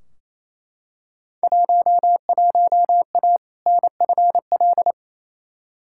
Morse code, heard as a keyed tone, about 28 wpm, 700 Hz. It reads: 11A NFL
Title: Field Day Exchanges Encore - 28wpm